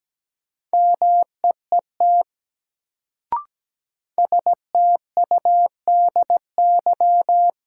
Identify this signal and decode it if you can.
Morse code, heard as a keyed tone, about 17 wpm, 700 Hz. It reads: MEET STUDY
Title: Sets of 5 Words - New Word Drill - 17wpm